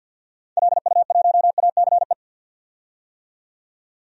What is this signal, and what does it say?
Morse code, heard as a keyed tone, about 53 wpm, 700 Hz. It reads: LU1ACE
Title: DXCC-100 Call Signs - 6 Characters - 50wpm